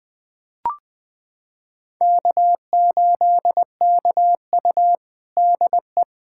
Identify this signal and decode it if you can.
Morse code, heard as a keyed tone, about 20 wpm, 700 Hz. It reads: K8KU DE
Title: USA Callsign DE Callsign - 4 Characters - 20wpm